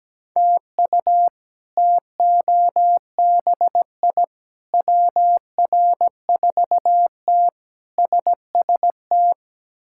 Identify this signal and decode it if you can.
Morse code, heard as a keyed tone, about 17 wpm, 700 Hz. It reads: TU TOBI WR4T SST
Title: K1USN SST QSO Practice - 17wpm